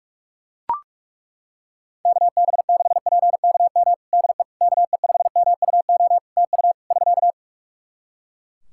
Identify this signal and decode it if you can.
Morse code, heard as a keyed tone, about 42 wpm, 700 Hz. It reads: KB6PKM DE KE5MUO TU <SK>
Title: Practice QSO - Exchange 4 - Ending - 45wpm